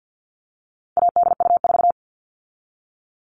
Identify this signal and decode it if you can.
Morse code, heard as a keyed tone, about 50 wpm, 700 Hz. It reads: ADU4
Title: All Cut Numbers - 4-Digit 50wpm